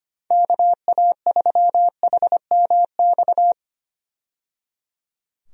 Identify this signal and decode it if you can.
Morse code, heard as a keyed tone, about 25 wpm, 700 Hz. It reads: KA3HMX
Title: USA Call Signs - 6 Characters - 25wpm